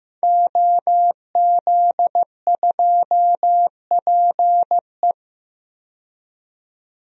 Morse code, heard as a keyed tone, about 15 wpm, 700 Hz. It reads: OZ2PE
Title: DXCC-100 Call Signs - 5 Characters - 15wpm